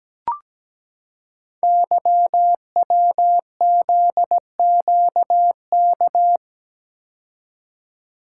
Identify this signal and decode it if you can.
Morse code, heard as a keyed tone, about 17 wpm, 700 Hz. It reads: YWZQK
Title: Code Groups - 5 Letters - 17wpm